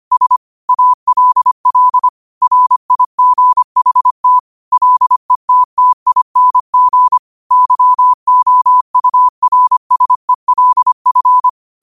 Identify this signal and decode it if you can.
Morse code, heard as a keyed tone, about 25 wpm, 1000 Hz. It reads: S ALL RIGHT LETTING YOURSELF
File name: qotdfiles / 25wpm